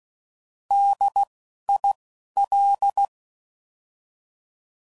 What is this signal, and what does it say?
Morse code, heard as a keyed tone, about 16 wpm, 800 Hz. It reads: D I L
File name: Prefixes / By First Letter / Ds